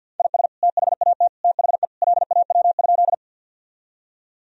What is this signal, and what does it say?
Morse code, heard as a keyed tone, about 51 wpm, 700 Hz. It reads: IS THAT THE LAW?
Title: Sentences from Top 1000 Words - Review Part 3 - 50wpm